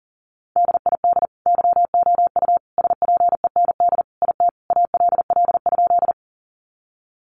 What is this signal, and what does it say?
Morse code, heard as a keyed tone, about 37 wpm, 700 Hz. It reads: DID YOU SPEND IT ALL?
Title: Sentences from Top 800 Words 40wpm